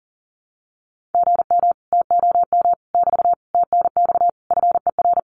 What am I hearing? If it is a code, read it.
Morse code, heard as a keyed tone, about 40 wpm, 700 Hz. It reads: GM TOM <BT> TNX FER
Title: Common QSO Phrases Encore 40wpm